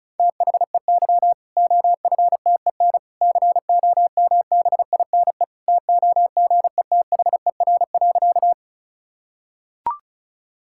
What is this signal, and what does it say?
Morse code, heard as a keyed tone, about 34 wpm, 700 Hz. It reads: THEY OFTEN COMBINE TOGETHER.
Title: Sentences from Top 1600 Words 35wpm